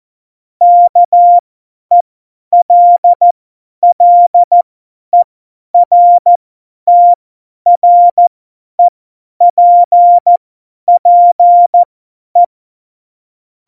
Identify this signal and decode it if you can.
Morse code, heard as a keyed tone, about 14 wpm, spaced as for 10 wpm, 700 Hz. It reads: KELLERTREPPE